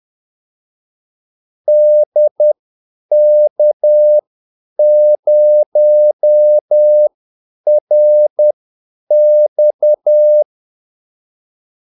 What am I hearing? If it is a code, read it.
Morse code, heard as a keyed tone, about 10 wpm, spaced as for 8 wpm, 600 Hz. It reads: DK0RX
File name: Callsigns-10-8wpm